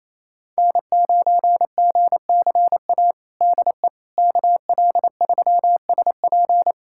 Morse code, heard as a keyed tone, about 28 wpm, 700 Hz. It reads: N9GCA DE KL3SP
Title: USA Callsign DE Callsign - 5 Characters - 28wpm